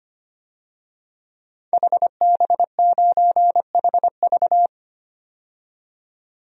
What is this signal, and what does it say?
Morse code, heard as a keyed tone, about 25 wpm, 700 Hz. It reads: HB9HV
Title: DXCC-100 Call Signs - 5 Characters - 25wpm